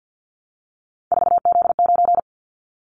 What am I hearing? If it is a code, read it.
Morse code, heard as a keyed tone, about 50 wpm, 700 Hz. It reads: 4Z9
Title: Callsign Prefixes for Top 50 DXCC - 50wpm